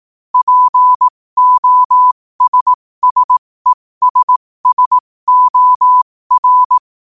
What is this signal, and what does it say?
Morse code, heard as a keyed tone, about 18 wpm, spaced as for 15 wpm, 1000 Hz. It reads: POSSESSOR
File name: 15wpm